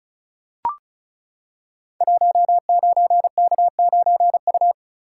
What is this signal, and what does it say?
Morse code, heard as a keyed tone, about 36 wpm, 690 Hz. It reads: 19K9U
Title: Code Groups - 5 Letter:Numbers - 35wpm